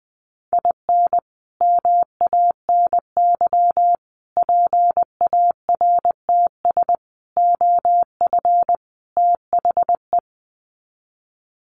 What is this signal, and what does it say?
Morse code, heard as a keyed tone, about 20 wpm, 700 Hz. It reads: IN MANY PARTS OF THE
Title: Sets of 5 Words Encore 20wpm